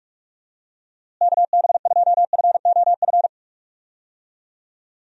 Morse code, heard as a keyed tone, about 42 wpm, 700 Hz. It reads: KB2FOF
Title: USA Call Signs - 6 Characters - Repeated 2x - 45wpm